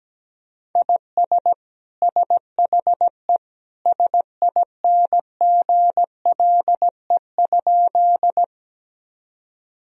Morse code, heard as a keyed tone, about 17 wpm, 700 Hz. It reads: IS SHE SINGLE?
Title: Sentences from Top 1000 Words - Review Part 4 - 17wpm